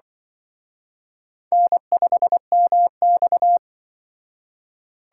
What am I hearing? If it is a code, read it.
Morse code, heard as a keyed tone, about 23 wpm, 700 Hz. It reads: N5MX
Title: Speed Racing - USA 1x2 Call Signs 20wpm